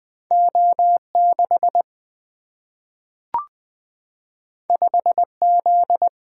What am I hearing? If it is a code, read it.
Morse code, heard as a keyed tone, about 20 wpm, 700 Hz. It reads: O6 5Z
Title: Single-Letter Single-Number 20wpm